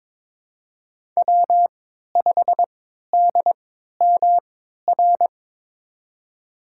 Morse code, heard as a keyed tone, about 22 wpm, spaced as for 12 wpm, 700 Hz. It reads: W5DMR
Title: USA Call Signs - 5 Characters - 3x Character Spacing - 22wpm